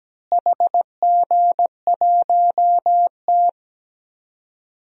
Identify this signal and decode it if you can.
Morse code, heard as a keyed tone, about 17 wpm, 700 Hz. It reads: HG1T
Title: DXCC-100 Call Signs - 4 Characters - Repeated 3x - 17wpm